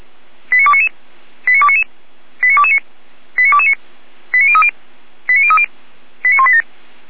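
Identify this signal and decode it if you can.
Five-tone selective call (ZVEI), heard as quick runs of five tones, tones of about 70 ms. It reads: 89299 89299 89229 89229 89030 89030 89187
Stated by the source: ZVEI five-tone selective call received on SDR.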